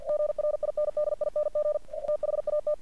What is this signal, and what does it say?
Morse code, heard as a keyed tone, about 48 wpm, 600 Hz. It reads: GRINDING THAT
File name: archived / CRSnet